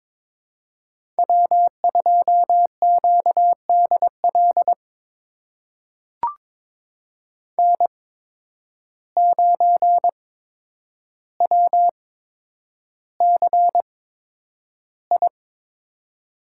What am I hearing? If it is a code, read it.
Morse code, heard as a keyed tone, about 22 wpm, spaced as for 9 wpm, 700 Hz. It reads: W2QDL N9WCI
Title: USA Call Signs - 5 Characters - 8x Character Spacing - 22wpm